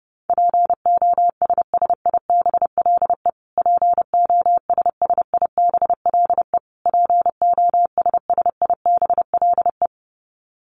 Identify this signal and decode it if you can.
Morse code, heard as a keyed tone, about 28 wpm, 700 Hz. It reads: POSSIBLE POSSIBLE POSSIBLE
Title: Top 500 Words Repeated 30wpm